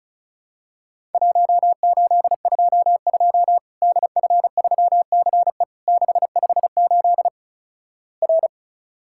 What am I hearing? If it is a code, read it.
Morse code, heard as a keyed tone, about 36 wpm, 700 Hz. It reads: 1822 DF3CE 658 R
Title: Worked All Europe DX Contest - QTC Exchange - 35wpm